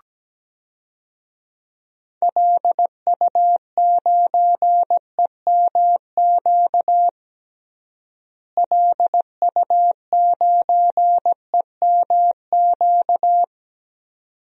Morse code, heard as a keyed tone, about 17 wpm, 700 Hz. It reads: LU9EMQ LU9EMQ
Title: DXCC-100 Call Signs - 6 Characters - Repeated 2x - 17wpm